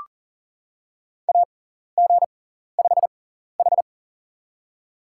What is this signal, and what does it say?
Morse code, heard as a keyed tone, about 40 wpm, spaced as for 14 wpm, 700 Hz. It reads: AG5H
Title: USA Call Signs - 6x Character Spacing - 40wpm